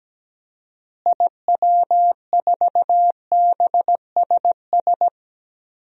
Morse code, heard as a keyed tone, about 17 wpm, 700 Hz. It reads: IW4BSS
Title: DXCC-100 Call Signs - 6 Characters - Repeated 3x - 17wpm